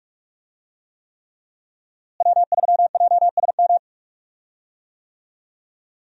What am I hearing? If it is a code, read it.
Morse code, heard as a keyed tone, about 46 wpm, 700 Hz. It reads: W3JSM
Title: USA Call Signs - 5 Characters - Repeated 3x - 45wpm